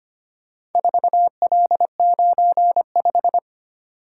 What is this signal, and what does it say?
Morse code, heard as a keyed tone, about 25 wpm, 700 Hz. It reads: 4L95
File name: practice / 25wpm / Four Letter-Number Combinations - 25wpm